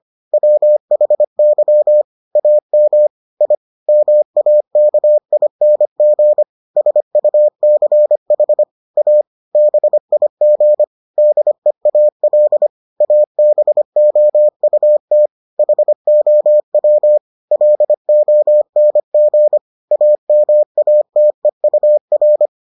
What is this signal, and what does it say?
Morse code, heard as a keyed tone, about 25 wpm, 600 Hz. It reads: WHY AM I MAKING SUCH A BIG DEAL ABOUT HOW LONG AMATEUR